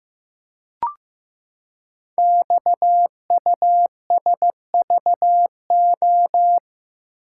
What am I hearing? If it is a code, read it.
Morse code, heard as a keyed tone, about 15 wpm, 700 Hz. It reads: XUSVO
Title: Code Groups - 5 Letter:Numbers - 15wpm